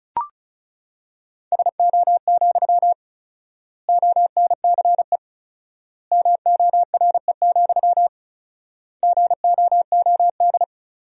Morse code, heard as a keyed tone, about 36 wpm, 700 Hz. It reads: SO, ONCE MORE, GOOD
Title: 4x Word-Spacing - Sentences from Top 300 Words - 35wpm